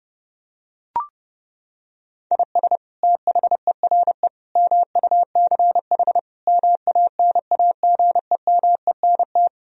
Morse code, heard as a keyed tone, about 30 wpm, 700 Hz. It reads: IS THERE MUCH MANAGEMENT
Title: Sentences from Top 1000 Words - Review Part 4 - 30wpm